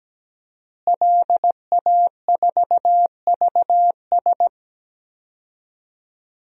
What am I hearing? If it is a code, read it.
Morse code, heard as a keyed tone, about 17 wpm, 700 Hz. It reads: LA4VS
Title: DXCC-100 Call Signs - 5 Characters - Repeated 2x - 17wpm